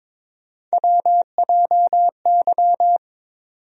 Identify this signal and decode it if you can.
Morse code, heard as a keyed tone, about 22 wpm, 700 Hz. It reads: WJY
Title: USA Call Sign Postfixes - 3 Letters 22wpm